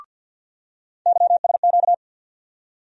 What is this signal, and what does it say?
Morse code, heard as a keyed tone, about 53 wpm, 700 Hz. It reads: YSQ
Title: USA Call Sign Postfixes - 3 Letters 50wpm